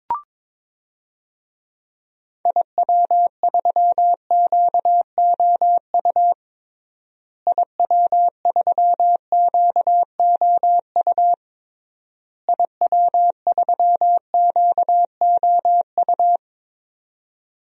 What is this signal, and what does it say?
Morse code, heard as a keyed tone, about 22 wpm, 700 Hz. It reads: IW3QOU IW3QOU IW3QOU
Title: DXCC-100 Call Signs - 6 Characters - Repeated 3x - 22wpm